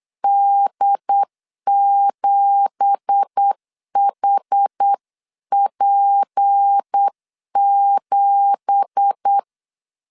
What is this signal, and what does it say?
Morse code, heard as a keyed tone, about 8 wpm, 800 Hz. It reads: D7HP7
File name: 8wpm